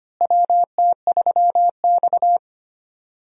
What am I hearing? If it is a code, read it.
Morse code, heard as a keyed tone, about 25 wpm, 700 Hz. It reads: WT3X
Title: USA Call Signs - 4 Characters - Repeated 2x - 25wpm